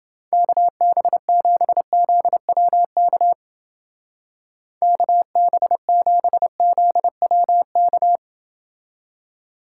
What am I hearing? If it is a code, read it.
Morse code, heard as a keyed tone, about 30 wpm, 700 Hz. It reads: KB7ZWK KB7ZWK